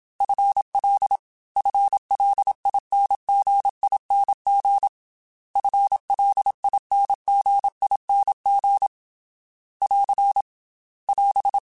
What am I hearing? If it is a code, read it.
Morse code, heard as a keyed tone, about 27 wpm, 800 Hz. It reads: FL FLINGING FLINGING <AR> <AS>
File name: FL / FL mp3s / FL Prefixes 3x2 Twice @25